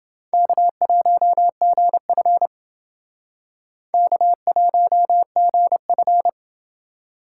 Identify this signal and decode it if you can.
Morse code, heard as a keyed tone, about 29 wpm, 700 Hz. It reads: K1GF K1GF